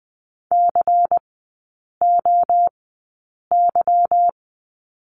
Morse code, heard as a keyed tone, about 20 wpm, spaced as for 8 wpm, 700 Hz. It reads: COY
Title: 152 Lesson 25e - Letter p - Three Characters 20wpm